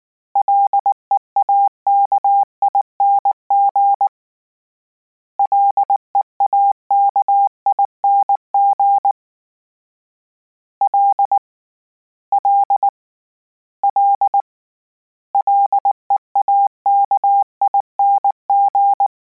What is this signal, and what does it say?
Morse code, heard as a keyed tone, about 19 wpm, 800 Hz. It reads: LEAKING LEAKING L L L LEAKING
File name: Combinations / Combinations-mp3s / Combinations@18